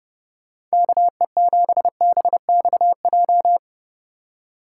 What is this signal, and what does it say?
Morse code, heard as a keyed tone, about 37 wpm, 700 Hz. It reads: KE7BXJ
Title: Speed Racing - USA Call Signs 30wpm